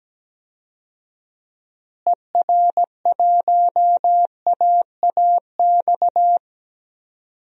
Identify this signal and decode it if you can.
Morse code, heard as a keyed tone, about 17 wpm, 700 Hz. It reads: ER1AAX
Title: DXCC-100 Call Signs - 6 Characters - Repeated 2x - 17wpm